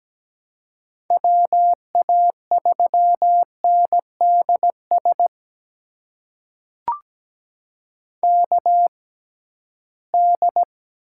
Morse code, heard as a keyed tone, about 17 wpm, spaced as for 13 wpm, 700 Hz. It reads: WA3NDS KD
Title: USA Call Signs - 6 Characters - 6x Character Spacing - 17wpm